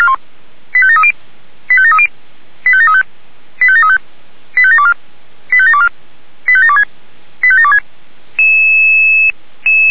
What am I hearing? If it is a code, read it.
Five-tone selective call (ZVEI), heard as quick runs of five tones, tones of about 70 ms. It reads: 85730 85730 85735 85735 85724 85724 85726 85726